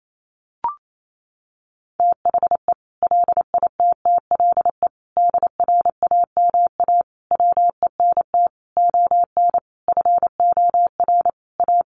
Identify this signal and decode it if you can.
Morse code, heard as a keyed tone, about 28 wpm, 700 Hz. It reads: THE LITTLE DRAMA WENT ON FOR A
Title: Sentences from Top 1500 Words 28wpm